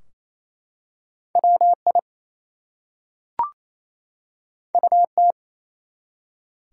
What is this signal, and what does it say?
Morse code, heard as a keyed tone, about 28 wpm, 700 Hz. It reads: WI UT
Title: US States and Canadian Provinces 28wpm